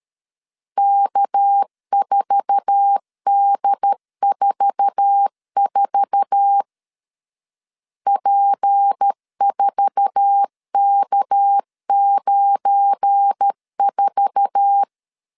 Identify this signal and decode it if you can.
Morse code, heard as a keyed tone, about 13 wpm, 800 Hz. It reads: K4D44 P4K94
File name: LEZIONI / 8 YJ45 / 8 YJ4 12wpm 2151